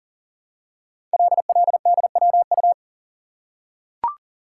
Morse code, heard as a keyed tone, about 38 wpm, 700 Hz. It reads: LLDWU
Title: Code Groups - 5 Letter:Numbers - 40wpm